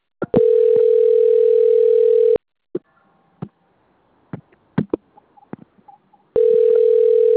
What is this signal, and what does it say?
Ringback tone (North America), 440 + 480 Hz.